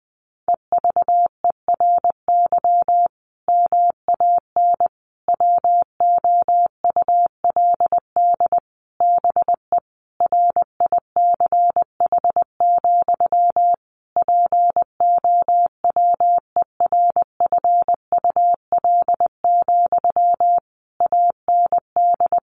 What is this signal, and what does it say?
Morse code, heard as a keyed tone, about 20 wpm, 700 Hz. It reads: EVERY MAN WOULD BE RICH, POWERFUL, AND
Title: Sentences from Top 1500 Words 20wpm